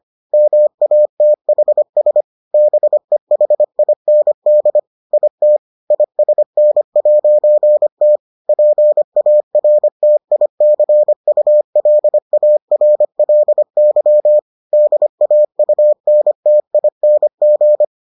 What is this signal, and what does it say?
Morse code, heard as a keyed tone, about 25 wpm, 600 Hz. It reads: MATHS BEHIND IT ISN'T PARTICULARLY DAUNTING